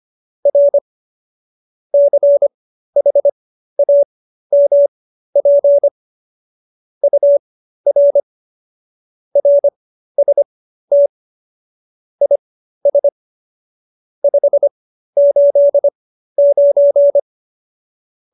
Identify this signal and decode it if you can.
Morse code, heard as a keyed tone, about 25 wpm, spaced as for 13 wpm, 570 Hz. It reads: R CHAMP UR RST IS 589
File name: qso208b 13